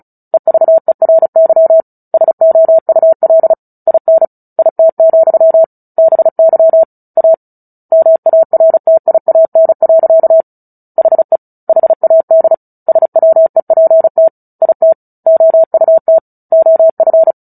Morse code, heard as a keyed tone, about 34 wpm, 650 Hz. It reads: EVERY SOUL IN IT, BY A MARTIAN. HE HAD SWEPT IT OUT OF
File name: Book Two / 09-Chapter 9 - 35 WPM